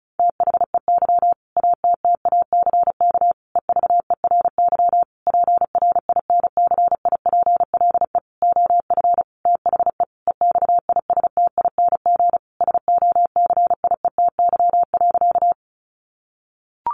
Morse code, heard as a keyed tone, about 34 wpm, 700 Hz. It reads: THEY ATTACK EVERY PRINCIPLE OF THE EXISTING SOCIETY.
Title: Sentences from Top 1500 Words 35wpm